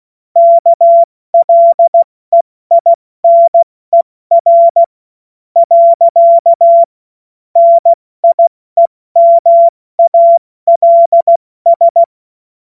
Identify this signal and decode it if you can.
Morse code, heard as a keyed tone, about 16 wpm, spaced as for 14 wpm, 680 Hz. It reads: KLEINER. NIEMALS
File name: Morsekurs / Witze / Witz 17. August 2021, 70 F80